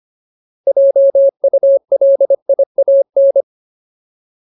Morse code, heard as a keyed tone, about 25 wpm, 550 Hz. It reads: JULIAN
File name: Ultimate Word List / 25WPM Ultimate